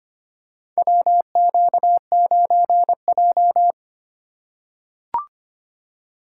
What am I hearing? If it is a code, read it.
Morse code, heard as a keyed tone, about 25 wpm, 700 Hz. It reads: WQ9J
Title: USA Call Signs - 4 Characters - 4x Character Spacing - 25wpm